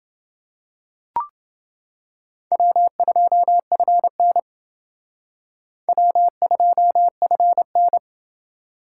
Morse code, heard as a keyed tone, about 28 wpm, 700 Hz. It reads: W2FN W2FN